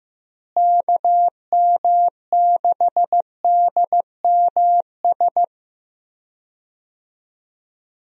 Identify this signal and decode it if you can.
Morse code, heard as a keyed tone, about 15 wpm, 700 Hz. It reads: KM6DMS
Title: USA Call Signs - 6 Characters - Repeated 3x - 15wpm